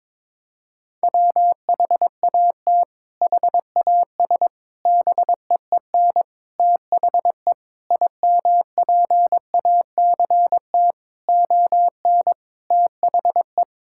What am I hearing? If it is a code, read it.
Morse code, heard as a keyed tone, about 22 wpm, 700 Hz. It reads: WHAT HAS BEEN THE IMPACT ON THE
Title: Sentences from Top 1000 Words - Review Part 6 - 22wpm